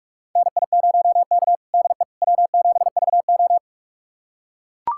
Morse code, heard as a keyed tone, about 41 wpm, 700 Hz. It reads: NI0K DE W7VO
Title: USA Callsign DE Callsign - 4 Characters - 45wpm